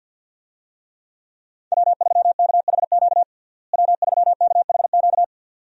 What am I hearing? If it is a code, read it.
Morse code, heard as a keyed tone, about 48 wpm, 700 Hz. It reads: W3KHQ W3KHQ
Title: USA Call Signs - 5 Characters - Repeated 2x - 50wpm